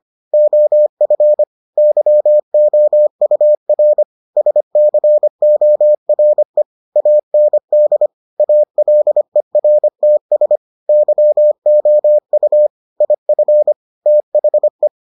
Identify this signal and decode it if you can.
Morse code, heard as a keyed tone, about 25 wpm, 600 Hz. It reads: OF YOUR SCORE AND ALERTS YOU IF THE